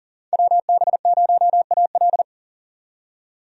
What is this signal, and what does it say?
Morse code, heard as a keyed tone, about 40 wpm, 700 Hz. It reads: WB0AL
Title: USA Call Signs - 5 Characters - 40wpm